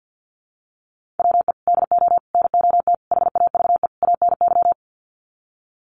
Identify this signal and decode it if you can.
Morse code, heard as a keyed tone, about 50 wpm, 700 Hz. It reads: WE DO NOT HAVE ANY
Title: Sets of 5 Words 50wpm